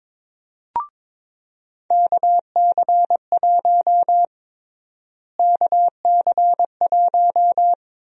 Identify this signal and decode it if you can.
Morse code, heard as a keyed tone, about 23 wpm, 700 Hz. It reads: KC1 KC1
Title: USA Call Sign Prefixes - Repeated 2x - 22wpm